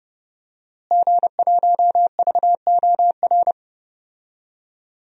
Morse code, heard as a keyed tone, about 32 wpm, 700 Hz. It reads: G1VOR
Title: DXCC-100 Call Signs - 5 Characters - Repeated 2x - 30wpm